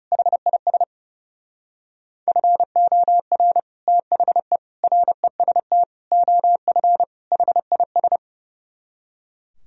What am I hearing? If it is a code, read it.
Morse code, heard as a keyed tone, about 30 wpm, 700 Hz. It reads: HIS FOR THE REST OF HIS
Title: Speed-Racing - Sets of 5 Words 30wpm